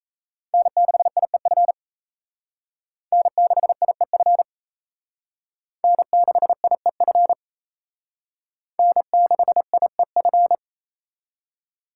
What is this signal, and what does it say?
Morse code, heard as a keyed tone, about 36 wpm, 700 Hz. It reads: N6IEF N6IEF N6IEF N6IEF